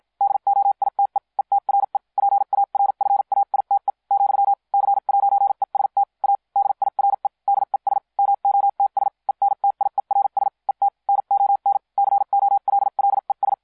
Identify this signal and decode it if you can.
Morse code, heard as a keyed tone, about 56 wpm, spaced as for 45 wpm, 800 Hz. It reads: DOITE ETRE PARFAITE , C'EST A DIRE DES MOTS ENTIERS ET NON COLLES